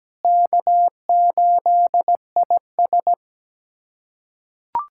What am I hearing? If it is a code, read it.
Morse code, heard as a keyed tone, about 17 wpm, 700 Hz. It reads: K8IS
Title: USA Callsign DE Callsign - 4 Characters - 17wpm